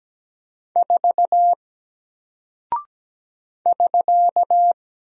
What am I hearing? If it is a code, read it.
Morse code, heard as a keyed tone, about 17 wpm, 700 Hz. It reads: 4 <SK>
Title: Single Character (Letter:Number:Punctuation:Prosign) 17wpm